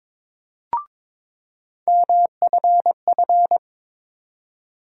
Morse code, heard as a keyed tone, about 23 wpm, 700 Hz. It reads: MFF
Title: USA Call Sign Postfixes - 3 Letters 22wpm